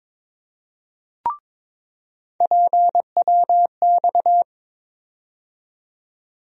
Morse code, heard as a keyed tone, about 22 wpm, 700 Hz. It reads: PWX
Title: USA Call Sign Postfixes - 3 Letters 22wpm